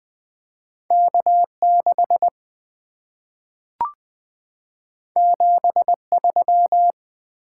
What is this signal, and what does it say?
Morse code, heard as a keyed tone, about 20 wpm, 700 Hz. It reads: K6 73
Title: Two Letter-Number Combinations - 20wpm